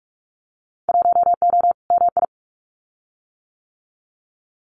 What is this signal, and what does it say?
Morse code, heard as a keyed tone, about 46 wpm, 700 Hz. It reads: 1O MI
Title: Winter Field Day Exchanges 45wpm